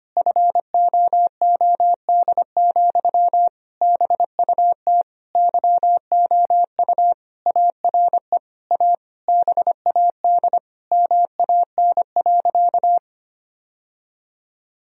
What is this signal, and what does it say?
Morse code, heard as a keyed tone, about 25 wpm, 700 Hz. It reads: FOOD, BUT YOU ARE A BAD MAN.
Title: Sentences from Top 1000 Words - Review Part 1 - 25wpm